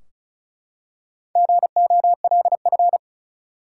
Morse code, heard as a keyed tone, about 35 wpm, 700 Hz. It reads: GOLF
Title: Four Letter Words - 35wpm